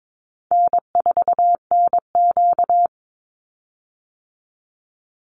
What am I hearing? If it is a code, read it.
Morse code, heard as a keyed tone, about 22 wpm, 700 Hz. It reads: N4NQ
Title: Call Signs Repeated 3x - 22wpm